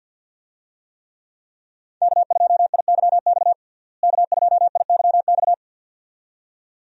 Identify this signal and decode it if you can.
Morse code, heard as a keyed tone, about 51 wpm, 700 Hz. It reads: K2IYX K2IYX
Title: USA Call Signs - 5 Characters - Repeated 2x - 50wpm